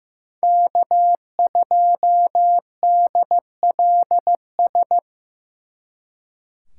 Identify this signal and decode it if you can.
Morse code, heard as a keyed tone, about 17 wpm, 700 Hz. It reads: K2DLS
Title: Speed Racing - USA 1x3 Call Signs 15wpm